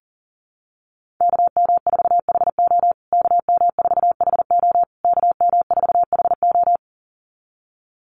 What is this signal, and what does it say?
Morse code, heard as a keyed tone, about 40 wpm, 700 Hz. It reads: KM4HO KM4HO KM4HO
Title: USA Call Signs Repeated 3x 40wpm